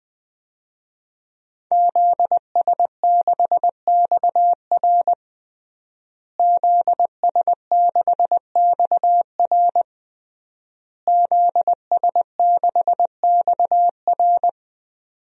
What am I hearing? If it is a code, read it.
Morse code, heard as a keyed tone, about 20 wpm, 700 Hz. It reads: ZS6XR ZS6XR ZS6XR
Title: DXCC-100 Call Signs - 5 Characters - Repeated 3x - 20wpm